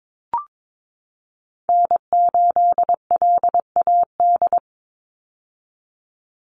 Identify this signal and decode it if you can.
Morse code, heard as a keyed tone, about 22 wpm, 700 Hz. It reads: N8LAD
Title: USA Call Signs 22wpm